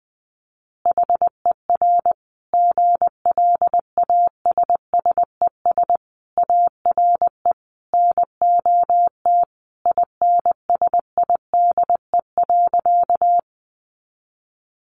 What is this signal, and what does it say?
Morse code, heard as a keyed tone, about 20 wpm, 700 Hz. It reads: HER GLASSES ARE NOT INSIDE.
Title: Additional Easy Phrases 20wpm